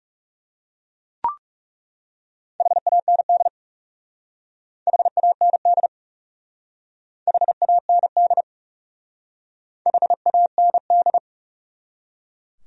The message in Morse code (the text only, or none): HAND HAND HAND HAND